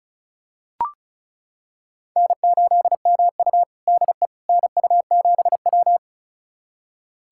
Morse code, heard as a keyed tone, about 36 wpm, 700 Hz. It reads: N8MU DE NU7W
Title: USA Callsign DE Callsign - 4 Characters - 35wpm